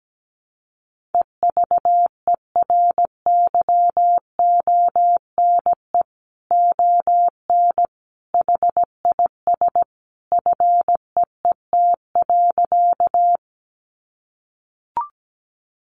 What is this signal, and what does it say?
Morse code, heard as a keyed tone, about 17 wpm, 700 Hz. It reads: EVERYONE ON HIS FEET.
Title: A Sound Of Thunder 17wpm